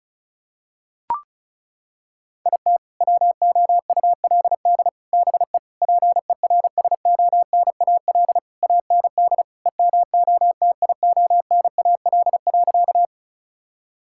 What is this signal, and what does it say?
Morse code, heard as a keyed tone, about 36 wpm, 700 Hz. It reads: IT WOULD BE PERSONAL AND EMOTIONAL.
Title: Sentences from Top 1000 Words - Review Part 5 - 35wpm